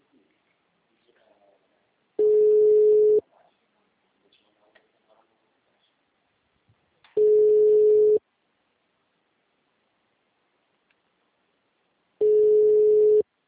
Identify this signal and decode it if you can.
Ringback tone (Europe, 425 Hz).